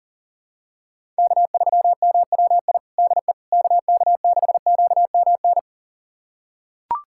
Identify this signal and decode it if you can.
Morse code, heard as a keyed tone, about 37 wpm, 700 Hz. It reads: K3MWI DE KK6QMN
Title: USA Callsign DE Callsign 40wpm